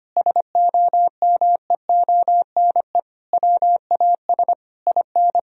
Morse code, heard as a keyed tone, about 25 wpm, 700 Hz. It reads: SOMEONE WAS IN
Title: Sentences from Top 1000 Words - Review Part 4 - 25wpm